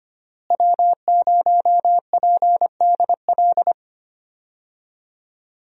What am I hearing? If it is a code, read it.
Morse code, heard as a keyed tone, about 25 wpm, 700 Hz. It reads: W0PDL
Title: USA Call Signs - 5 Characters - Repeated 3x - 25wpm